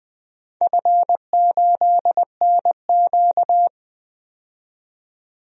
Morse code, heard as a keyed tone, about 20 wpm, 700 Hz. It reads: F8NQ
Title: DXCC-100 Call Signs - 4 Characters - 20wpm